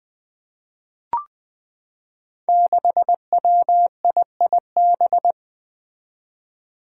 Morse code, heard as a keyed tone, about 20 wpm, 700 Hz. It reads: 6WIIB
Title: Five Challenging Letter-Number Combinations 20wpm